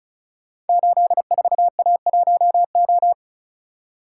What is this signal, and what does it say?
Morse code, heard as a keyed tone, about 36 wpm, 690 Hz. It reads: 84A1O
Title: Code Groups - 5 Letter:Numbers - 35wpm